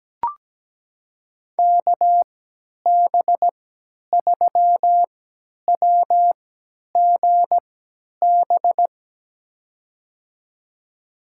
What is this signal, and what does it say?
Morse code, heard as a keyed tone, about 17 wpm, spaced as for 10 wpm, 700 Hz. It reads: KB3WGB
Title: USA Call Signs - 6 Characters - 3x Character Spacing - 17wpm